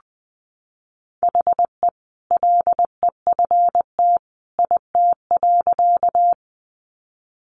Morse code, heard as a keyed tone, about 20 wpm, 700 Hz. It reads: HE LEFT IT.
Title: A Sound Of Thunder 20wpm